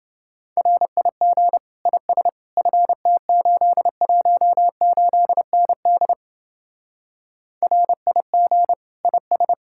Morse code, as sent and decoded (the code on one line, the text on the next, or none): .-. .. --. / .. ... / ..-. - ---.. .---- ---.. -. -.. / .-. .. --. / .. ...
RIG IS FT818ND RIG IS